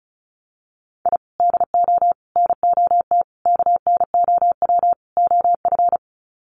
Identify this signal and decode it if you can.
Morse code, heard as a keyed tone, about 34 wpm, 700 Hz. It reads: I DO NOT KNOW OF
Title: Sets of 5 Words Encore2 35wpm